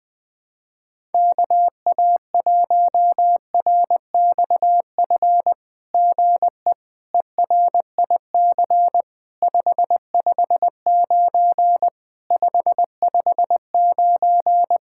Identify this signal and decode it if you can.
Morse code, heard as a keyed tone, about 20 wpm, 700 Hz. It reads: KA1RXF GE ERIC 559 559
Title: SOTA QSO Practice 20wpm